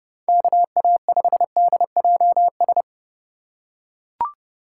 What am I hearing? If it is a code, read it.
Morse code, heard as a keyed tone, about 29 wpm, 700 Hz. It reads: KA5DJS